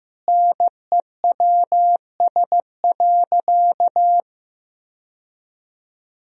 Morse code, heard as a keyed tone, about 15 wpm, 700 Hz. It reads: NEWS.
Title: Sentences from Top 1000 Words - Review Part 4 - 15wpm